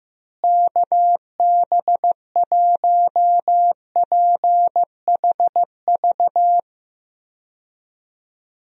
Morse code, heard as a keyed tone, about 15 wpm, 700 Hz. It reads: KB1PHV
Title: USA Call Signs - 6 Characters - Repeated 2x - 15wpm